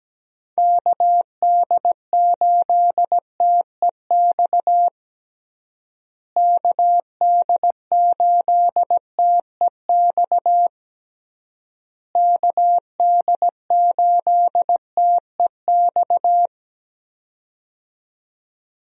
Morse code, heard as a keyed tone, about 17 wpm, 700 Hz. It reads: KD8TEX KD8TEX KD8TEX
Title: USA Call Signs - 6 Characters - Repeated 3x - 17wpm